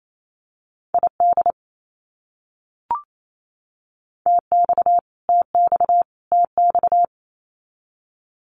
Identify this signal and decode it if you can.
Morse code, heard as a keyed tone, about 28 wpm, 700 Hz. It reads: ID TX TX TX
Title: US States Repeated 3x 28wpm